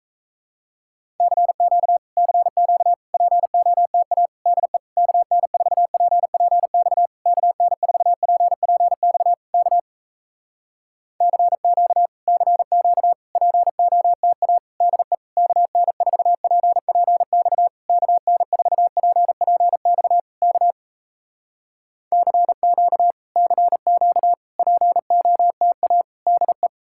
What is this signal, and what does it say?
Morse code, heard as a keyed tone, about 38 wpm, 700 Hz. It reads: CQ CQ POTA DE KN4PPX KN4PPX K CQ CQ POTA DE KN4PPX KN4PPX K CQ CQ POTA DE